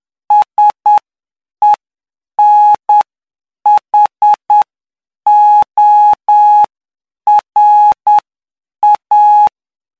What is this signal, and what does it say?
Morse code, heard as a keyed tone, about 9 wpm, spaced as for 7 wpm, 840 Hz. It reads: SENHORA
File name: cw / py2lcd / 05-PY2LCD-Li��o 14